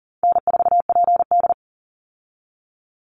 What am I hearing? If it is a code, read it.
Morse code, heard as a keyed tone, about 40 wpm, 700 Hz. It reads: N4PD
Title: Easy Call Signs Repeated 3x - 40wpm